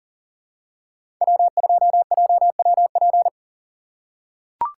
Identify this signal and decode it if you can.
Morse code, heard as a keyed tone, about 40 wpm, 700 Hz. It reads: W2JWP
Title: USA Call Signs - 5 Characters - 4x Character Spacing - 40wpm